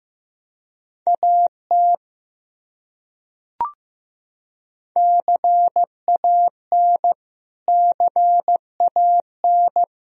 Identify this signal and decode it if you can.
Morse code, heard as a keyed tone, about 15 wpm, 700 Hz. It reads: AT CAN CAN
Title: Top 100 Words Repeated 15wpm